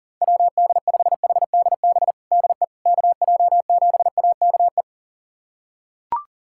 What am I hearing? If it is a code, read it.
Morse code, heard as a keyed tone, about 40 wpm, 700 Hz. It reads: WD5HDB DE KJ7AKE
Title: USA Callsign DE Callsign - 6 Characters - 40wpm